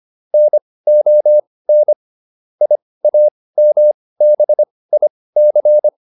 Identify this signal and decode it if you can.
Morse code, heard as a keyed tone, about 25 wpm, spaced as for 18 wpm, 600 Hz. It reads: NON IAMBIC